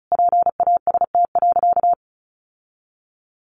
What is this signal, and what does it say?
Morse code, heard as a keyed tone, about 34 wpm, 700 Hz. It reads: PAST.
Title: Sentences from Top 1300 Words 35wpm